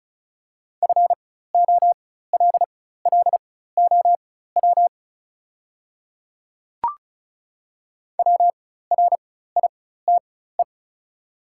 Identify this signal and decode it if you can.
Morse code, heard as a keyed tone, about 36 wpm, spaced as for 16 wpm, 700 Hz. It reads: FOLLOW WRITE
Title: 4x Character Spacing - Top 200 Words - 35wpm